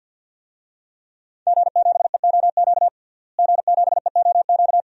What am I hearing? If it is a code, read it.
Morse code, heard as a keyed tone, about 52 wpm, 700 Hz. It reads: G7EOQ G7EOQ
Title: DXCC-100 Call Signs - 5 Characters - Repeated 2x - 50wpm